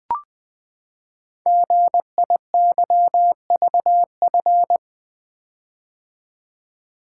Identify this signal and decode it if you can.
Morse code, heard as a keyed tone, about 20 wpm, 700 Hz. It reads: GIYVF
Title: Code Groups - 5 Letters - 20wpm